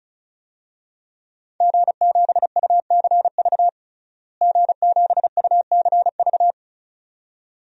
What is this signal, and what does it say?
Morse code, heard as a keyed tone, about 36 wpm, 700 Hz. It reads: G7UCV G7UCV
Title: DXCC-100 Call Signs - 5 Characters - Repeated 2x - 35wpm